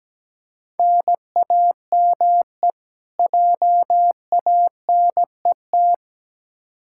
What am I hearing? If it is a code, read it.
Morse code, heard as a keyed tone, about 17 wpm, 700 Hz. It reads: NAME JANET
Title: K1USN SST QSO Practice 17wpm